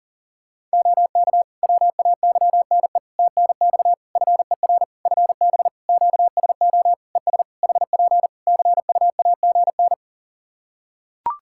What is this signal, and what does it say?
Morse code, heard as a keyed tone, about 40 wpm, 700 Hz. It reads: OK WAYNE TNX FER FB QSO ES HP CUAGN
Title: Practice QSO - Exchange 4 - Ending - 40wpm